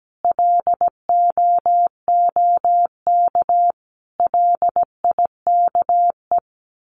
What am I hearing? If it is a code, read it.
Morse code, heard as a keyed tone, about 17 wpm, 700 Hz. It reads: LOOK LIKE
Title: Sets of 5 Words Encore2 17wpm